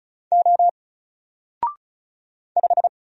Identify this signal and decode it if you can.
Morse code, heard as a keyed tone, about 34 wpm, 700 Hz. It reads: O 5
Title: Single Letter-Number Alt 35wpm